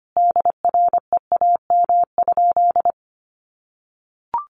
Morse code, heard as a keyed tone, about 25 wpm, 700 Hz. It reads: DREAM?
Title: Sentences from Top 1500 Words 25wpm